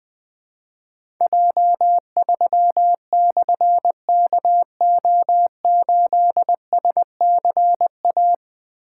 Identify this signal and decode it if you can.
Morse code, heard as a keyed tone, about 20 wpm, 700 Hz. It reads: J3/KO8SCA
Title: Difficult Call Signs 20wpm